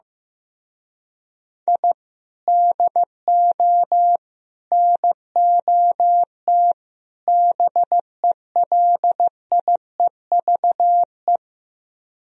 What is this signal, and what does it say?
Morse code, heard as a keyed tone, about 15 wpm, 700 Hz. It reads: I DO NOT BELIEVE
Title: Speed-Racing - Sets of 4 Words 15wpm